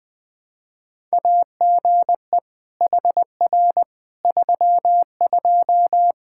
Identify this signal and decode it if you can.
Morse code, heard as a keyed tone, about 20 wpm, 700 Hz. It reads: AGE HR 32
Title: Practice QSO - Exchange 3 - More Info - 20wpm